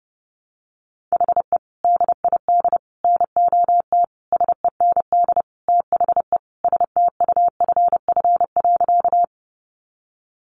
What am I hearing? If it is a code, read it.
Morse code, heard as a keyed tone, about 30 wpm, 700 Hz. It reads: HE DID NOT SEND THE STUFF.
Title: Sentences from Top 1500 Words 30wpm